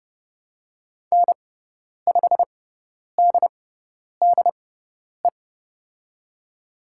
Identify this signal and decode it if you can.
Morse code, heard as a keyed tone, about 30 wpm, spaced as for 10 wpm, 700 Hz. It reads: N5DDE
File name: Lesson 18g - Letter m - Call Signs 30wpm